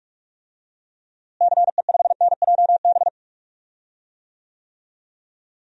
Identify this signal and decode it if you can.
Morse code, heard as a keyed tone, about 42 wpm, 700 Hz. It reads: CE5NJB
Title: DXCC-100 Call Signs - 6 Characters - Repeated 3x - 45wpm